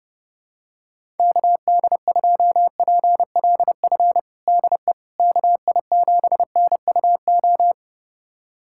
Morse code, heard as a keyed tone, about 32 wpm, 700 Hz. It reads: KD2PLF DE KI7NUO
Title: USA Callsign DE Callsign - 6 Characters - 30wpm